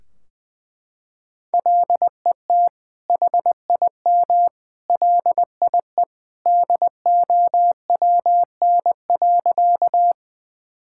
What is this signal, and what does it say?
Morse code, heard as a keyed tone, about 20 wpm, 700 Hz. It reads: LET HIM LIE DOWN.
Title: Sentences from Top 1600 Words 20wpm